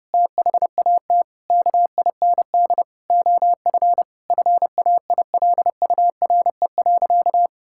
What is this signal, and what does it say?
Morse code, heard as a keyed tone, about 30 wpm, 700 Hz. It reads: THAT KIND OF FAILURE.
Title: Sentences from Top 1000 Words - Review Part 6 - 30wpm